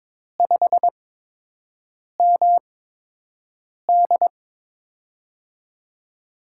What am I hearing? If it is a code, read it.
Morse code, heard as a keyed tone, about 22 wpm, spaced as for 6 wpm, 700 Hz. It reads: 5MD
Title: USA Call Signs - 8x Character Spacing - 22wpm